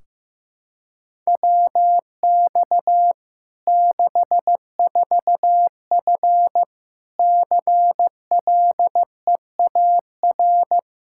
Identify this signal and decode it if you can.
Morse code, heard as a keyed tone, about 15 wpm, 700 Hz. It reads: WX 64F CLEAR
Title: Practice QSO - Exchange 2 - Info - 15wpm